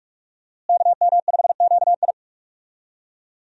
Morse code, heard as a keyed tone, about 46 wpm, 700 Hz. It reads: KM5QI
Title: USA Call Signs - 5 Characters - Repeated 3x - 45wpm